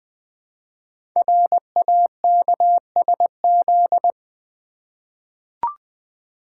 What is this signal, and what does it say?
Morse code, heard as a keyed tone, about 20 wpm, 700 Hz. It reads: RAKSZ
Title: Code Groups - 5 Letter:Numbers - 20wpm